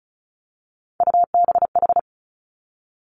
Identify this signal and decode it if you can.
Morse code, heard as a keyed tone, about 36 wpm, 700 Hz. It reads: UBH
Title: Three Challenging Letter-Number Combinations 35wpm